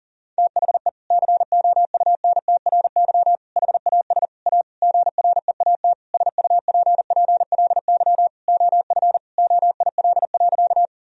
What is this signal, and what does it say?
Morse code, heard as a keyed tone, about 37 wpm, 700 Hz. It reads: THE COUNTRY HAS A GREAT SUPPLY OF OIL.
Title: Sentences from Top 1000 Words - Review Part 4 - 40wpm